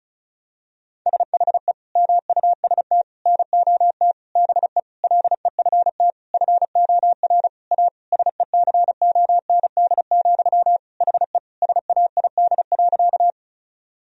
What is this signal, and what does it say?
Morse code, heard as a keyed tone, about 36 wpm, 700 Hz. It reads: SHE MUST NOT BE LEFT FOR A SECOND, HE SAID.
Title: Sentences from Top 1000 Words - Review Part 1 - 35wpm